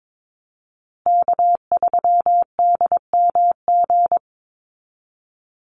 Morse code, heard as a keyed tone, about 22 wpm, 700 Hz. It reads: K3DMG
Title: Call Signs Repeated 2x - 22wpm